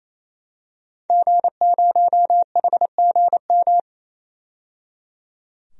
Morse code, heard as a keyed tone, about 28 wpm, 700 Hz. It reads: G0HGM
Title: DXCC-100 Call Signs - 5 Characters - 28wpm